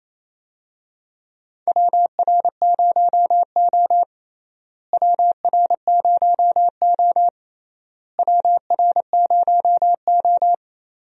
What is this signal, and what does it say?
Morse code, heard as a keyed tone, about 28 wpm, 700 Hz. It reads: WR0O WR0O WR0O
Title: USA Call Signs - 4 Characters - Repeated 3x - 28wpm